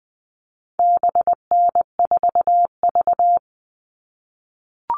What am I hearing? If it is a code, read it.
Morse code, heard as a keyed tone, about 20 wpm, 700 Hz. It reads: BN4V
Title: All Cut Numbers - 4-Digit 20wpm